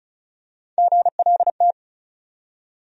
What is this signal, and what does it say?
Morse code, heard as a keyed tone, about 36 wpm, 700 Hz. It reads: GLT
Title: USA Call Sign Postfixes - 3 Letters 35wpm